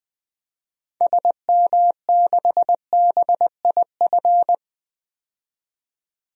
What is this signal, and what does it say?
Morse code, heard as a keyed tone, about 20 wpm, 700 Hz. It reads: SM6BIF
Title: DXCC-100 Call Signs - 6 Characters - 20wpm